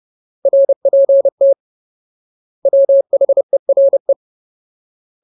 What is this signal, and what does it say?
Morse code, heard as a keyed tone, about 30 wpm, 550 Hz. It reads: RPT WHERE
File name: Ultimate Word List / 30WPM Ultimate